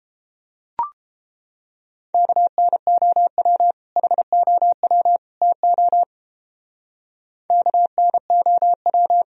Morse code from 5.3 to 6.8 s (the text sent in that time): TO